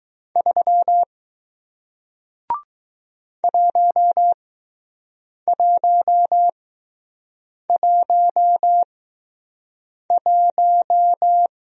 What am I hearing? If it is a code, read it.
Morse code, heard as a keyed tone, about 20 wpm, spaced as for 6 wpm, 700 Hz. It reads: 3 1111